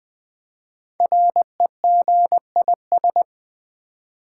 Morse code, heard as a keyed tone, about 20 wpm, 700 Hz. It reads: REGIS
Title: Build A Word - 5 Letters At A Time - 12-Letter Words - 20wpm